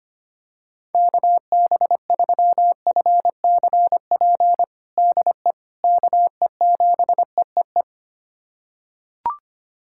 Morse code, heard as a keyed tone, about 25 wpm, 700 Hz. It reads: KB3FCP DE KE7EEE
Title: USA Callsign DE Callsign 25wpm